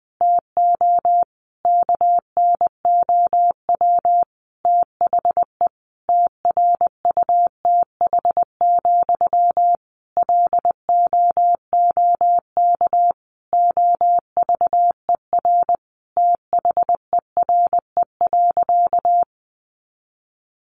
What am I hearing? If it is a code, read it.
Morse code, heard as a keyed tone, about 19 wpm, 700 Hz. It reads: TO KNOW THE TRUTH, LOOK OVER THERE.
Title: Sentences from Top 800 Words 20wpm